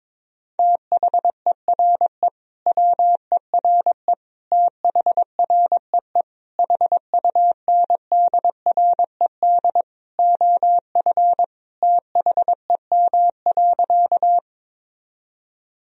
Morse code, heard as a keyed tone, about 22 wpm, 700 Hz. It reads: THERE WERE THREE HUNDRED OF THEM.
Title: Sentences from Top 1000 Words - Review Part 2 - 22wpm